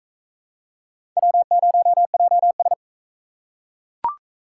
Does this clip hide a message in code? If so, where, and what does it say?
Yes: Morse code from 1.2 to 2.7 s, heard as a keyed tone, about 38 wpm, 700 Hz. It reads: W0JS